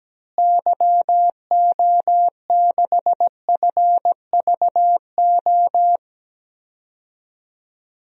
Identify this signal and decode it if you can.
Morse code, heard as a keyed tone, about 17 wpm, 700 Hz. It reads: YO6FVO
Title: DXCC-100 Call Signs - 6 Characters - Repeated 3x - 17wpm